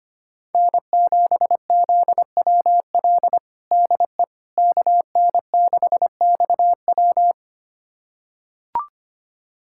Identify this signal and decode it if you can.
Morse code, heard as a keyed tone, about 25 wpm, 700 Hz. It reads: N7ZWL DE KN6XW
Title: USA Callsign DE Callsign - 5 Characters - 25wpm